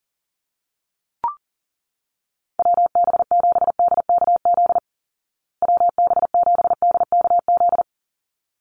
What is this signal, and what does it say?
Morse code, heard as a keyed tone, about 40 wpm, 700 Hz. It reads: WB7DKZ WB7DKZ
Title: Call Signs Repeated 3x - 40wpm